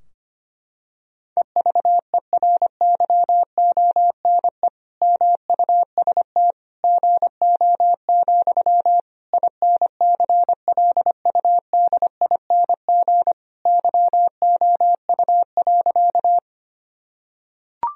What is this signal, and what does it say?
Morse code, heard as a keyed tone, about 25 wpm, 700 Hz. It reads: EVERYONE MUST GO, INCLUDING YOU.
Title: Sentences from Top 1000 Words - Review Part 4 - 25wpm